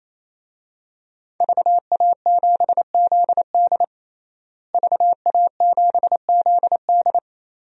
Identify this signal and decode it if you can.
Morse code, heard as a keyed tone, about 27 wpm, 700 Hz. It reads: VA7ZD VA7ZD
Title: DXCC-100 Call Signs - 5 Characters - Repeated 3x - 28wpm